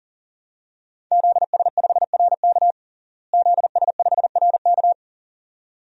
Morse code, heard as a keyed tone, about 40 wpm, 700 Hz. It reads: ZS5RK ZS5RK
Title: DXCC-100 Call Signs - 5 Characters - Repeated 2x - 40wpm